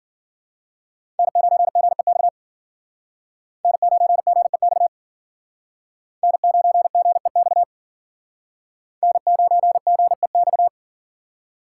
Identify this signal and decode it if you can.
Morse code, heard as a keyed tone, about 51 wpm, 700 Hz. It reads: N9GEX N9GEX N9GEX N9GEX